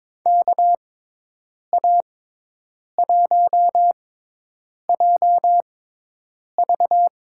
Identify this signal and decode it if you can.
Morse code, heard as a keyed tone, about 22 wpm, spaced as for 8 wpm, 700 Hz. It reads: KA1JV
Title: USA Call Signs - 6 Characters - 6x Character Spacing - 22wpm